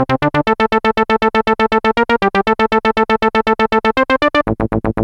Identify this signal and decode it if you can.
There is no code.